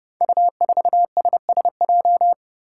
Morse code, heard as a keyed tone, about 30 wpm, 700 Hz. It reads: U4SSJ
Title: Five Challenging Letter-Number Combinations 30wpm